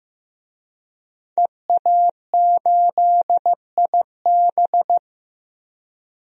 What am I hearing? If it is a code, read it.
Morse code, heard as a keyed tone, about 15 wpm, 700 Hz. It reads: EA8IB
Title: DXCC-100 Call Signs - 5 Characters - 15wpm